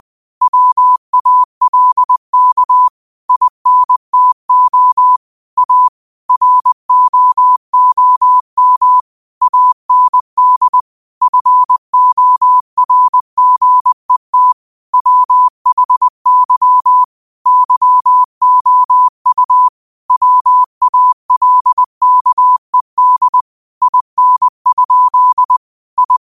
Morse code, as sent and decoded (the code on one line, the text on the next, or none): .-- .- .-.. -.- / .. -. - --- / .- / .-. --- --- -- / .- -. -.. / ..-. --- .-. --. - / .-- .... -.-- / -.-- --- ..- / .-- .- .-.. -.- . -.. / .. -. ..--.. / ..
WALK INTO A ROOM AND FORGET WHY YOU WALKED IN? I